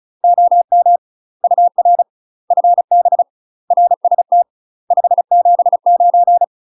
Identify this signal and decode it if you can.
Morse code, heard as a keyed tone, about 36 wpm, 700 Hz. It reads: OM UR FB RST 579
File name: UK MASTER QSOs TRAINING / 35WPM / UK MASTER QSOs 0022